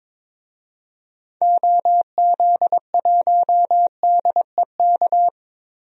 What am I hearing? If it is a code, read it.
Morse code, heard as a keyed tone, about 22 wpm, 700 Hz. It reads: OZ1DEK
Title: DXCC-100 Call Signs - 6 Characters - 22wpm